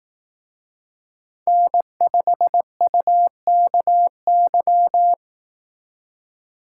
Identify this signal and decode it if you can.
Morse code, heard as a keyed tone, about 18 wpm, 700 Hz. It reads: N5UKY
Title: Speed Racing - USA 1x3 Call Signs 15wpm